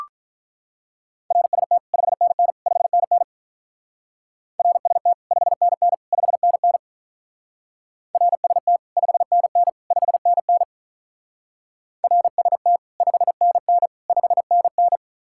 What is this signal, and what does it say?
Morse code, heard as a keyed tone, about 42 wpm, 700 Hz. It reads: RST 5NN 5NN RST 5NN 5NN RST 5NN 5NN RST 5NN 5NN
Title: Speed-Racing Common QSO Phrases Encore - 35wpm